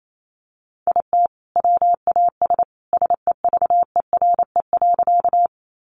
Morse code, heard as a keyed tone, about 28 wpm, 700 Hz. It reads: IT WAS SEVERE.
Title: Sentences from Top 1400 Words 28wpm